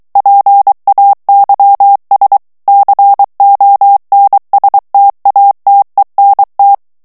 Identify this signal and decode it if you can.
Morse code, heard as a keyed tone, about 23 wpm, 800 Hz. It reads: PAYS CONSTATENT
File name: FRANCAIS TEXTE CLAIR 1 VITESSE 22 MOTS MINUTES